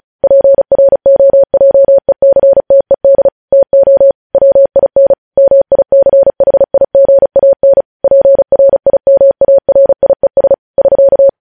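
Morse code, heard as a keyed tone, about 36 wpm, 550 Hz. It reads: PROJECTED TO WIN MICHIGAN PRIMARIES <SK>
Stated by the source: Morse news headlines practice at 35 words per minute.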